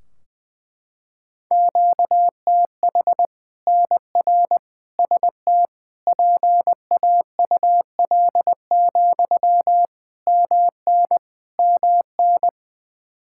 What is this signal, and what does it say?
Morse code, heard as a keyed tone, about 20 wpm, 700 Hz. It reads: QTH NR ST PAUL, MN MN
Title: Practice QSO - Exchange 1 - Beginning - 20wpm